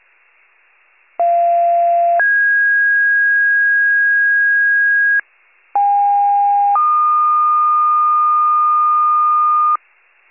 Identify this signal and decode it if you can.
Two-tone page, heard as a one-second tone, then a three-second tone, about 690 then 1650 Hz, about 810 then 1220 Hz. It is heard from 1.2 to 9.8 s.
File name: pager / 21 RESCUE / wcalert